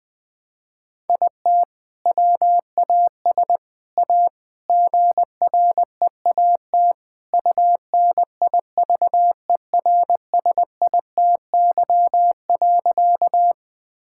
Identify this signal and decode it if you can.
Morse code, heard as a keyed tone, about 20 wpm, 700 Hz. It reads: IT WAS A GREAT UNIVERSITY.
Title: Sentences from Top 1000 Words - Review Part 5 - 20wpm